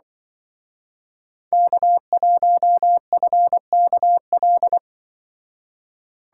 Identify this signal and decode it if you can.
Morse code, heard as a keyed tone, about 24 wpm, 700 Hz. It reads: K1FKL
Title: Speed Racing - USA 1x3 Call Signs 20wpm